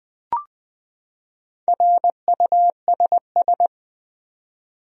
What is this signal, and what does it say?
Morse code, heard as a keyed tone, about 20 wpm, 700 Hz. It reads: RUSS
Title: USA Common Names - Max 4 Letters - 20wpm